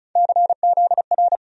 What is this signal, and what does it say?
Morse code, heard as a keyed tone, about 36 wpm, 700 Hz. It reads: CZR
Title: USA Call Sign Postfixes - 3 Letters 35wpm